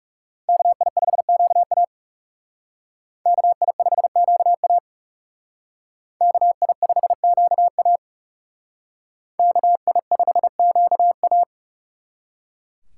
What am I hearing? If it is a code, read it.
Morse code, heard as a keyed tone, about 38 wpm, 700 Hz. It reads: KI5QA KI5QA KI5QA KI5QA